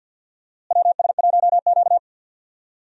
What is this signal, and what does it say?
Morse code, heard as a keyed tone, about 50 wpm, 700 Hz. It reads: WS1Q
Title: USA Call Signs - 4 Characters - Repeated 3x - 50wpm